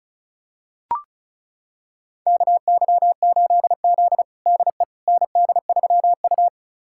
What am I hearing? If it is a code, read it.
Morse code, heard as a keyed tone, about 34 wpm, 700 Hz. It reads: KY8Z DE ND3U
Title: USA Callsign DE Callsign - 4 Characters - 35wpm